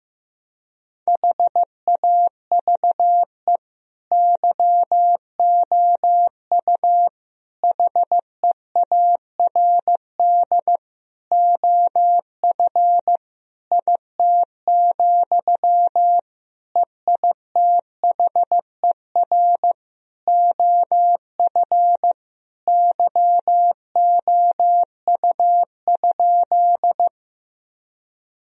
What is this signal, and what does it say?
Morse code, heard as a keyed tone, about 16 wpm, 700 Hz. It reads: HAVE YOU HEARD OF IT, EITHER OF YOU?
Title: Sentences from Top 1000 Words - Review Part 4 - 15wpm